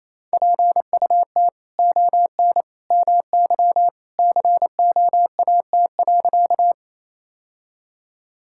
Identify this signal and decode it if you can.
Morse code, heard as a keyed tone, about 28 wpm, 700 Hz. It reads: PUT ON MY COAT.
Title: Sentences from Top 1600 Words 28wpm